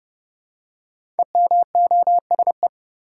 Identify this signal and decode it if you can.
Morse code, heard as a keyed tone, about 30 wpm, 700 Hz. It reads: EMOSE
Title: Code Groups - 5 Letters - 30wpm